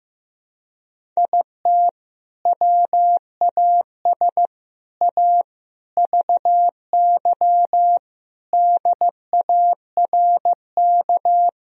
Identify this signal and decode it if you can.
Morse code, heard as a keyed tone, about 15 wpm, 700 Hz. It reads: IT WAS A VY DARK
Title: Sentences from Top 500 Words and Abbreviations 15wpm